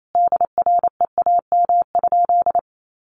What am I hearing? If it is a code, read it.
Morse code, heard as a keyed tone, about 28 wpm, 700 Hz. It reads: DREAM?
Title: Sentences from Top 1500 Words 28wpm